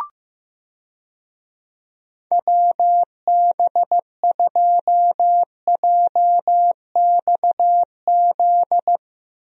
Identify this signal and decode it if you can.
Morse code, heard as a keyed tone, about 16 wpm, 700 Hz. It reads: WB2JXZ
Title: USA Call Signs - 6 Characters - 15wpm